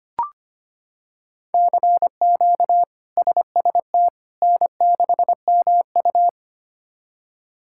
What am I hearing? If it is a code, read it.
Morse code, heard as a keyed tone, about 25 wpm, 700 Hz. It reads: CQ SST N6MU
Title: K1USN SST QSO Practice - 25wpm